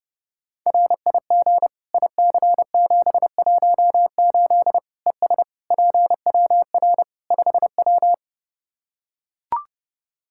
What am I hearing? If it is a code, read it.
Morse code, heard as a keyed tone, about 32 wpm, 700 Hz. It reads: RIG IC718 ES PWR 5W
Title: K1USN SST QSO Practice 30wpm